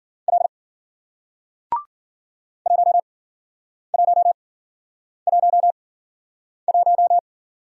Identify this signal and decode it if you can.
Morse code, heard as a keyed tone, about 54 wpm, spaced as for 9 wpm, 700 Hz. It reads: L 1111